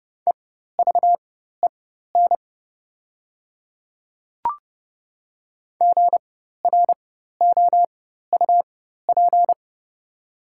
Morse code, heard as a keyed tone, about 30 wpm, spaced as for 14 wpm, 700 Hz. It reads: EVEN GROUP
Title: 4x Character Spacing - Top 500 Words - 30wpm